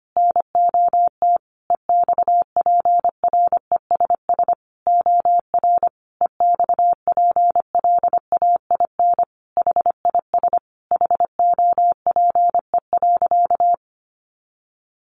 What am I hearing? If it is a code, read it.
Morse code, heard as a keyed tone, about 25 wpm, 700 Hz. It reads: NOT EXPRESS OR EXPLAIN HIS HOPE.
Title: Sentences from Top 1100 Words 25wpm